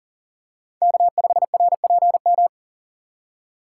Morse code, heard as a keyed tone, about 40 wpm, 700 Hz. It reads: K5RPM
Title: USA Call Signs - 5 Characters - 2x Character Spacing - 40wpm